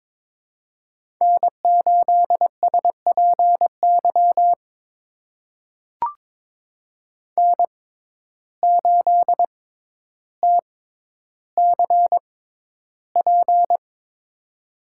Morse code, heard as a keyed tone, about 21 wpm, spaced as for 11 wpm, 700 Hz. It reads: N8SPY N8TCP
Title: USA Call Signs - 5 Characters - 6x Character Spacing - 22wpm